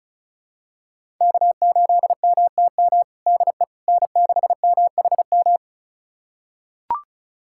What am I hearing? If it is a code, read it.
Morse code, heard as a keyed tone, about 34 wpm, 700 Hz. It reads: K8MTM DE N6MHM
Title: USA Callsign DE Callsign - 5 Characters - 35wpm